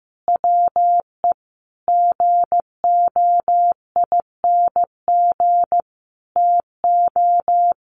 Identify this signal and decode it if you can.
Morse code, heard as a keyed tone, about 15 wpm, 700 Hz. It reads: WE GOING TO
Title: Sets of 5 Words 15wpm